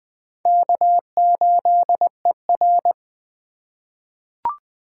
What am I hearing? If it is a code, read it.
Morse code, heard as a keyed tone, about 20 wpm, 700 Hz. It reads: K8ER
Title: USA Call Signs - 4 Characters - 3x Character Spacing - 20wpm